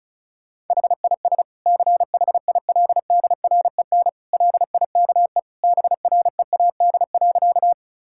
Morse code, heard as a keyed tone, about 34 wpm, 700 Hz. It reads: HIS CHILDREN LIKE BREAD.
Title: Sentences from Top 1600 Words 35wpm